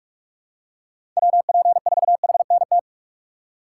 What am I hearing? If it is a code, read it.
Morse code, heard as a keyed tone, about 44 wpm, 700 Hz. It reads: WP4HNT
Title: USA Call Signs - 6 Characters - 6x Character Spacing - 45wpm